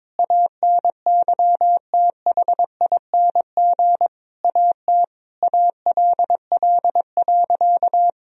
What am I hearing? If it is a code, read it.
Morse code, heard as a keyed tone, about 22 wpm, 700 Hz. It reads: ANYTHING AT ALL.
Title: Sentences from Top 1000 Words - Review Part 1 - 22wpm